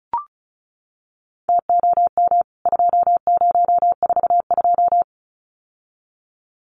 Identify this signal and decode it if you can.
Morse code, heard as a keyed tone, about 34 wpm, 700 Hz. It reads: TOM 2042
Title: CWT Exchange 35wpm